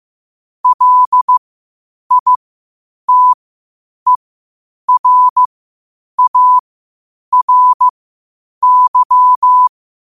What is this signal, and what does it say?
Morse code, heard as a keyed tone, about 15 wpm, spaced as for 8 wpm, 1000 Hz. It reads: LITERARY
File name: qotdfiles / 8wpm